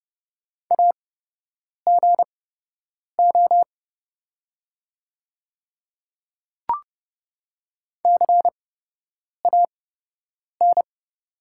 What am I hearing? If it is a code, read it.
Morse code, heard as a keyed tone, about 30 wpm, spaced as for 8 wpm, 700 Hz. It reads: AGO CAN
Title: 8x Character Spacing - Top 500 Words - 30wpm